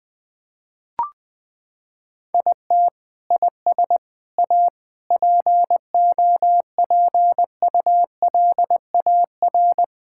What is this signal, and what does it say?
Morse code, heard as a keyed tone, about 20 wpm, 700 Hz. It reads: IT IS A POPULAR
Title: Sentences from Top 1000 Words - Review Part 5 - 20wpm